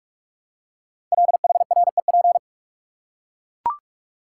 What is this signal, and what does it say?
Morse code, heard as a keyed tone, about 44 wpm, 700 Hz. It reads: LHREP
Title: Code Groups - 5 Letters - 45wpm